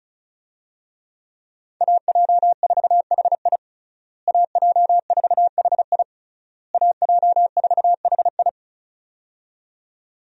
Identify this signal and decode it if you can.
Morse code, heard as a keyed tone, about 36 wpm, 700 Hz. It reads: AJ4HI AJ4HI AJ4HI
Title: USA Call Signs - 5 Characters - Repeated 3x - 35wpm